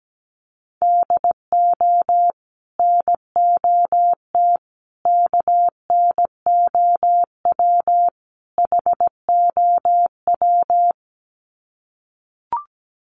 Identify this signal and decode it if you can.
Morse code, heard as a keyed tone, about 17 wpm, 700 Hz. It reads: DO NOT KNOW HOW
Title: Sets of 4 Words 17wpm